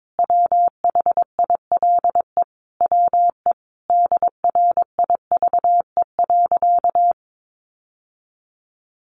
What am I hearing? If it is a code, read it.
Morse code, heard as a keyed tone, about 22 wpm, 700 Hz. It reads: WHILE WE DRIVE.
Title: Sentences from Top 1300 Words 22wpm